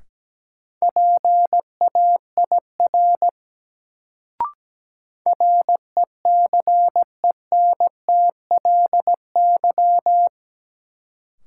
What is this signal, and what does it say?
Morse code, heard as a keyed tone, about 17 wpm, 700 Hz. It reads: PAIR RECENTLY
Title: Sentences from Top 1200 Words - New Word Drill Alt - 17wpm